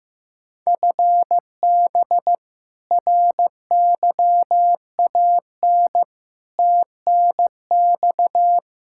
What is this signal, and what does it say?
Morse code, heard as a keyed tone, about 15 wpm, 700 Hz. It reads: FB RYAN TNX
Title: K1USN SST QSO Practice 15wpm